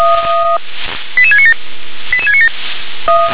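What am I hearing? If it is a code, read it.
Five-tone selective call (ZVEI), heard as quick runs of five tones, tones of about 70 ms. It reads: 88697 88697
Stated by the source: ZVEI five-tone selective call received on an SDR.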